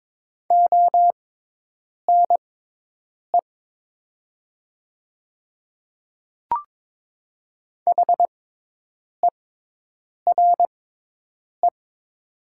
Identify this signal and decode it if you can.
Morse code, heard as a keyed tone, about 21 wpm, spaced as for 8 wpm, 700 Hz. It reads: ONE HERE
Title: 6x Character Spacing - Top 300 Words - 22wpm